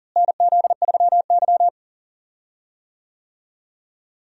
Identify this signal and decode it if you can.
Morse code, heard as a keyed tone, about 40 wpm, 700 Hz. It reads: NZ3Y
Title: USA Call Signs - 4 Characters - Repeated 2x - 40wpm